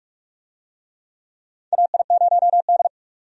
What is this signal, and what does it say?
Morse code, heard as a keyed tone, about 47 wpm, 700 Hz. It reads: AI0D
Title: USA Call Signs - 4 Characters - 45wpm